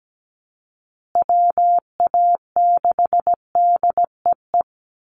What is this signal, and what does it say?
Morse code, heard as a keyed tone, about 17 wpm, 700 Hz. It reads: WA6DEE
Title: USA Call Signs Repeated 2x 17wpm